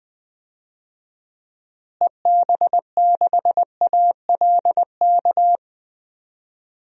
Morse code, heard as a keyed tone, about 20 wpm, 700 Hz. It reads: EB6ALK
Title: DXCC-100 Call Signs - 6 Characters - Repeated 3x - 20wpm